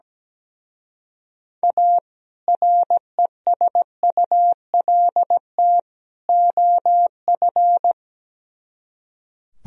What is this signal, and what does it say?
Morse code, heard as a keyed tone, about 17 wpm, 700 Hz. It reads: A RESULT OF
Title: Speed-Racing - Sets of 3 Words 17wpm